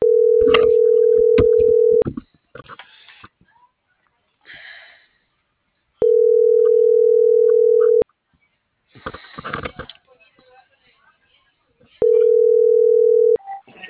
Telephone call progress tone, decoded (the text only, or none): ringback tone